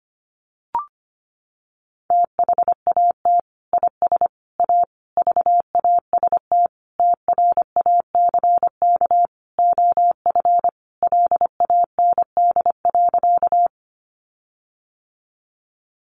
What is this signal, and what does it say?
Morse code, heard as a keyed tone, about 25 wpm, 700 Hz. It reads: THAT IS A VAST TRACK OF LAND.
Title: Sentences from Top 1300 Words 25wpm